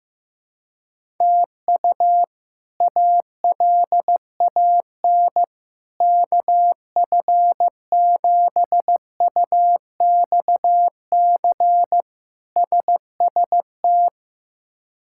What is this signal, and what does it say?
Morse code, heard as a keyed tone, about 15 wpm, 700 Hz. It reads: TU ALAN KF7UXC SST
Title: K1USN SST QSO Practice - 15wpm